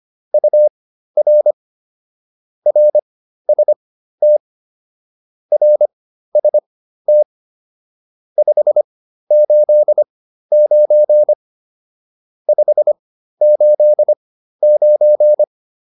Morse code, heard as a keyed tone, about 25 wpm, spaced as for 13 wpm, 600 Hz. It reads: UR RST RST 589 589